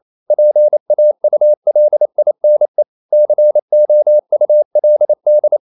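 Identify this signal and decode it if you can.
Morse code, heard as a keyed tone, about 28 wpm, 600 Hz. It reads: PAULINE COULD
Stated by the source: Morse code practice at 28 wpm, text of Ken Follett's novel Never in English, from CW med Gnister.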